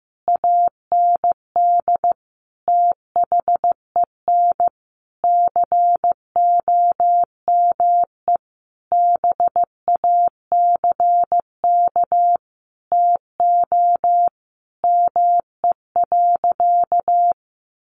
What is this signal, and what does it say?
Morse code, heard as a keyed tone, about 15 wpm, 700 Hz. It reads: AND THEN COME BACK TO ME.
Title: Sentences from Top 600 Words 15wpm